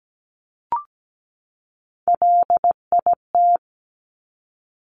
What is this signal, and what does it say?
Morse code, heard as a keyed tone, about 17 wpm, 690 Hz. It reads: LIT
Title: Top Three Letter Combinations in Words 17wpm